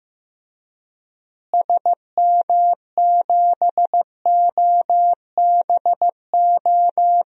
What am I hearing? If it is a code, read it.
Morse code, heard as a keyed tone, about 15 wpm, 700 Hz. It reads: SM7OBO
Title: DXCC-100 Call Signs - 6 Characters - Repeated 3x - 15wpm